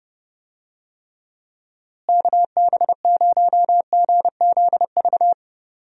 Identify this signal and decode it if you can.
Morse code, heard as a keyed tone, about 32 wpm, 700 Hz. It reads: KB0GZV
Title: USA Call Signs - 6 Characters - Repeated 3x - 30wpm